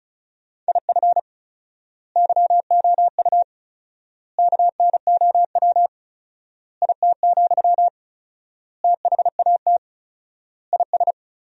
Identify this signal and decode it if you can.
Morse code, heard as a keyed tone, about 34 wpm, 700 Hz. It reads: IF YOU KNOW IT, THAT IS
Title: 4x Word-Spacing - Sentences from Top 400 Words - 35wpm